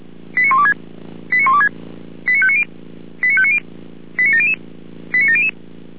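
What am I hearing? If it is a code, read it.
Five-tone selective call (ZVEI), heard as quick runs of five tones, tones of about 70 ms. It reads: 89126 89126 89590 89590 89700 89700
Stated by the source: ZVEI five-tone selective call 89115 received on an SDR.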